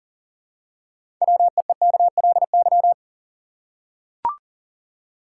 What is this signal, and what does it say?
Morse code, heard as a keyed tone, about 38 wpm, 700 Hz. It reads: WEEKLY
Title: Six Letter Words - 40wpm